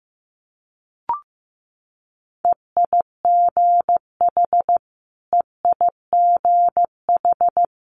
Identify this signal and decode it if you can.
Morse code, heard as a keyed tone, about 15 wpm, 700 Hz. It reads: EIGH EIGH
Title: Multi-letter Phonemes Repeated 3x 15wpm